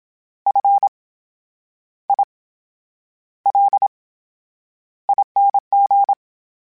Morse code, heard as a keyed tone, about 27 wpm, spaced as for 9 wpm, 800 Hz. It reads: FILING
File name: Special INGs / Special INGs-mp3s / VerySlowToFaster Ing @25